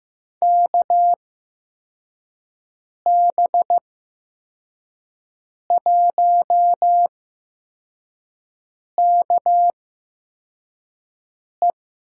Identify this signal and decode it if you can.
Morse code, heard as a keyed tone, about 15 wpm, spaced as for 4 wpm, 700 Hz. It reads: KB1KE
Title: USA Call Signs - 5 Characters - 8x Character Spacing - 15wpm